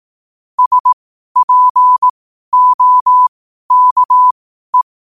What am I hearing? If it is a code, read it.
Morse code, heard as a keyed tone, about 18 wpm, spaced as for 12 wpm, 1000 Hz. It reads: SPOKE
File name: qotdfiles / 12wpm